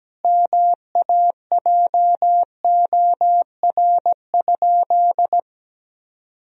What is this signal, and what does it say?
Morse code, heard as a keyed tone, about 17 wpm, 700 Hz. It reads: MAJOR?
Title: Sentences from Top 1000 Words - Review Part 4 - 17wpm